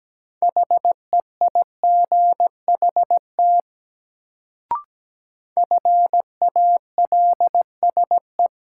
Morse code, heard as a keyed tone, about 17 wpm, 700 Hz. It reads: HEIGHT FALSE
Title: Sentences from Top 1600 Words - New Word Drill - 17wpm